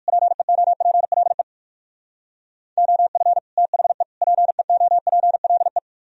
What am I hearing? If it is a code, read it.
Morse code, heard as a keyed tone, about 58 wpm, 700 Hz. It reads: PEOPLE OF THE PEOPLE